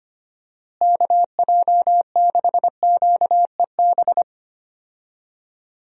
Morse code, heard as a keyed tone, about 25 wpm, 700 Hz. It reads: KJ6QEB
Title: USA Call Signs - 6 Characters - Repeated 2x - 25wpm